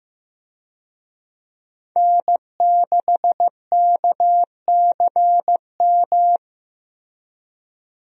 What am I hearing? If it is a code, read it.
Morse code, heard as a keyed tone, about 15 wpm, 700 Hz. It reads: N6KCM
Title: USA Call Signs - 5 Characters - Repeated 3x - 15wpm